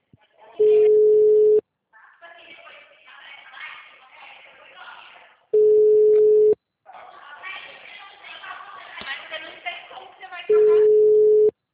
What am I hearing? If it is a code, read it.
Ringback tone (Europe, 425 Hz), with speech.